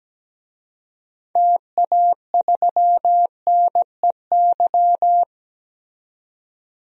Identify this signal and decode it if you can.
Morse code, heard as a keyed tone, about 17 wpm, 700 Hz. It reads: TA3NEY
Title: DXCC-100 Call Signs - 6 Characters - 17wpm